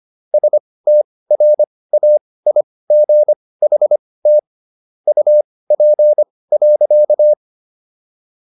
Morse code, heard as a keyed tone, about 25 wpm, spaced as for 18 wpm, 600 Hz. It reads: STRAIGHT UP.